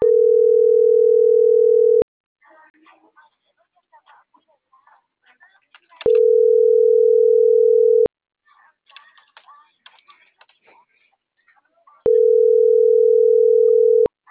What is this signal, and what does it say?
Ringback tone (North America), 440 + 480 Hz.